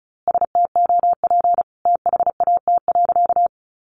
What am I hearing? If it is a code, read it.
Morse code, heard as a keyed tone, about 34 wpm, 700 Hz. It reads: STOP THAT.
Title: A Sound Of Thunder 35wpm